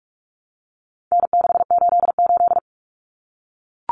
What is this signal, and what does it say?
Morse code, heard as a keyed tone, about 44 wpm, 700 Hz. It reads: N688
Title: Common Cut Numbers - 4-Digit 45wpm